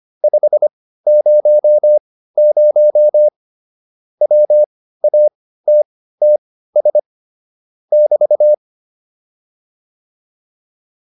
Morse code, heard as a keyed tone, about 25 wpm, spaced as for 15 wpm, 600 Hz. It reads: 500 WATTS <BT>